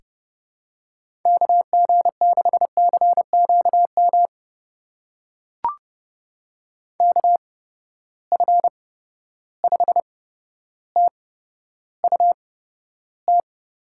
Morse code, heard as a keyed tone, about 32 wpm, spaced as for 13 wpm, 700 Hz. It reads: KG6CQM KF5TUT
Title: USA Call Signs - 8x Character Spacing - 30wpm